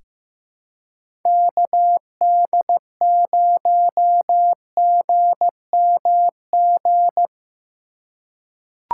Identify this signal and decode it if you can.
Morse code, heard as a keyed tone, about 15 wpm, 700 Hz. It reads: KD0GMG
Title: USA Call Signs - 6 Characters - 3x Character Spacing - 15wpm